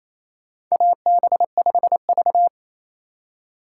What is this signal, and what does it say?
Morse code, heard as a keyed tone, about 28 wpm, 700 Hz. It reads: AB5V
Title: USA Call Signs - 4 Characters - 3x Character Spacing - 28wpm